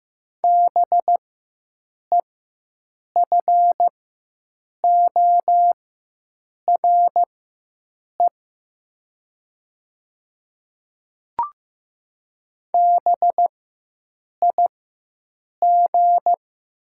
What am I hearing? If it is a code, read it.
Morse code, heard as a keyed tone, about 15 wpm, spaced as for 7 wpm, 700 Hz. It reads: BEFORE BIG
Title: 4x Character Spacing - Top 300 Words - 15wpm